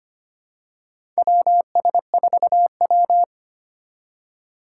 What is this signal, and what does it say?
Morse code, heard as a keyed tone, about 25 wpm, 700 Hz. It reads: WS4W
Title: USA Call Signs - 4 Characters - 3x Character Spacing - 25wpm